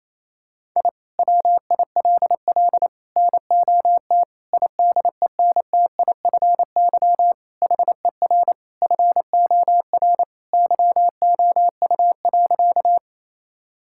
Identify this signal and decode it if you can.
Morse code, heard as a keyed tone, about 27 wpm, 700 Hz. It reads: I WILL NOT IDENTIFY HER FOR YOU.
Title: Sentences from Top 1000 Words - Review Part 6 - 28wpm